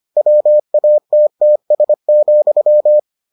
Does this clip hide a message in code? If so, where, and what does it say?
Yes: Morse code from 0.2 to 3.0 s, heard as a keyed tone, about 25 wpm, 600 Hz. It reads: WATTS,